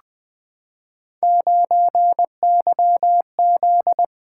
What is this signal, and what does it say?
Morse code, heard as a keyed tone, about 20 wpm, 700 Hz. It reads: 9YZ
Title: Three Letter-Number Combinations - 20wpm